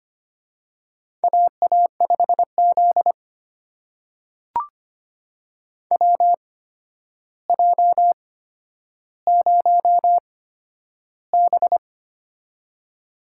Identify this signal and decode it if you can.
Morse code, heard as a keyed tone, about 25 wpm, spaced as for 11 wpm, 700 Hz. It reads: AA5Z WJ0B
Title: USA Call Signs - 4 Characters - 8x Character Spacing - 25wpm